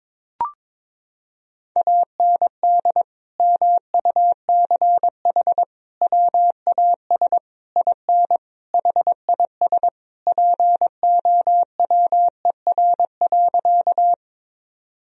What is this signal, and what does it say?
Morse code, heard as a keyed tone, about 22 wpm, 700 Hz. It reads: AND MUCH WAS IN HIS POWER.
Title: Sentences from Top 1000 Words - Review Part 2 - 22wpm